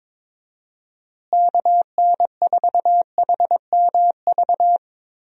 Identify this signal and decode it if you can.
Morse code, heard as a keyed tone, about 22 wpm, 700 Hz. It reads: KN4HMV
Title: USA Call Signs - 6 Characters - Repeated 3x - 22wpm